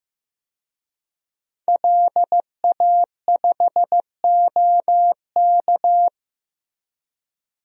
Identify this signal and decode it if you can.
Morse code, heard as a keyed tone, about 15 wpm, 700 Hz. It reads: LA5OK
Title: DXCC-100 Call Signs - 5 Characters - Repeated 3x - 15wpm